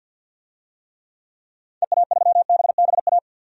Morse code, heard as a keyed tone, about 52 wpm, 700 Hz. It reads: EA3BBA
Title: DXCC-100 Call Signs - 6 Characters - 50wpm